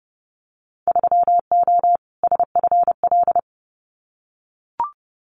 Morse code, heard as a keyed tone, about 30 wpm, 700 Hz. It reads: 3O SFL
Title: Winter Field Day Exchanges 30wpm